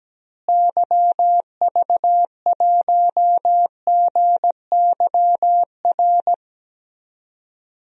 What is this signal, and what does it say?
Morse code, heard as a keyed tone, about 17 wpm, 700 Hz. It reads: YV1GYR
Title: DXCC-100 Call Signs - 6 Characters - Repeated 2x - 17wpm